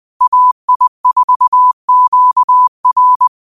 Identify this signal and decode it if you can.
Morse code, heard as a keyed tone, about 20 wpm, 1000 Hz. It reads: AI4QR
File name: qotdfiles / 20wpm